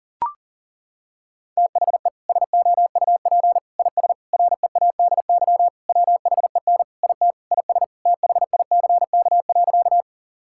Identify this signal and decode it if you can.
Morse code, heard as a keyed tone, about 37 wpm, 700 Hz. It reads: THE SOUP IS READY WHEN IT IS THICK.
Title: Sentences from Top 1600 Words 40wpm